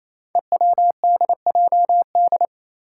Morse code, heard as a keyed tone, about 28 wpm, 700 Hz. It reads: EWDJD
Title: Five Challenging Letter-Number Combinations 28wpm